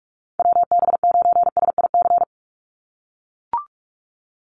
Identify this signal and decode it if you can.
Morse code, heard as a keyed tone, about 44 wpm, 700 Hz. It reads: WB9SIC
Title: USA Call Signs 45wpm